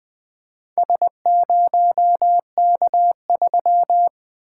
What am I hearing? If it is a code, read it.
Morse code, heard as a keyed tone, about 20 wpm, 700 Hz. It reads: S0K3
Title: Four Letter-Number Combinations - 20wpm